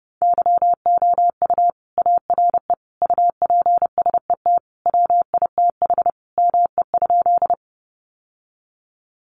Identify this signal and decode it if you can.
Morse code, heard as a keyed tone, about 30 wpm, 700 Hz. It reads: YOU ARE UPSET WITH ME?
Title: Sentences from Top 800 Words 30wpm